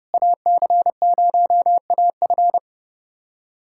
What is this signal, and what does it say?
Morse code, heard as a keyed tone, about 32 wpm, 700 Hz. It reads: AC0AF
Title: USA Callsign DE Callsign - 5 Characters - 30wpm